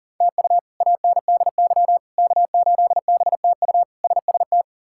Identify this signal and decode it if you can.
Morse code, heard as a keyed tone, about 40 wpm, 700 Hz. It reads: TU ANDY K8BTU SST
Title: K1USN SST QSO Practice - 40wpm